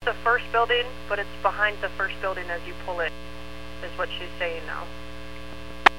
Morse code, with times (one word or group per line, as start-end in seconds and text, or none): none